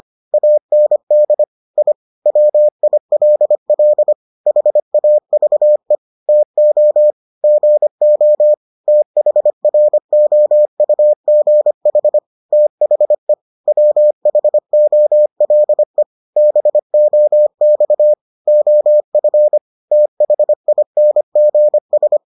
Morse code, heard as a keyed tone, about 25 wpm, 600 Hz. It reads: AND I WILL HAVE TO GO THROUGH THE WHOLE BOX OF THINGS